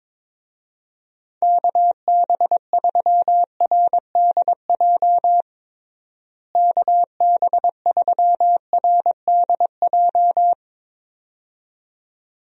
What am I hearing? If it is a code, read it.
Morse code, heard as a keyed tone, about 22 wpm, 700 Hz. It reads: KB3RDJ KB3RDJ
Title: USA Call Signs - 6 Characters - Repeated 2x - 22wpm